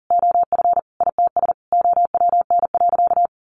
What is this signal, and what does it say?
Morse code, heard as a keyed tone, about 40 wpm, 700 Hz. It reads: OF ITS OWN.
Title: Sentences from Top 1200 Words 40wpm